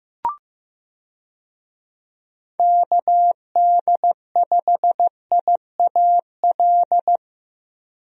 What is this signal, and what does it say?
Morse code, heard as a keyed tone, about 15 wpm, 700 Hz. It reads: KD5IAL
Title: USA Call Signs - 6 Characters - Repeated 2x - 15wpm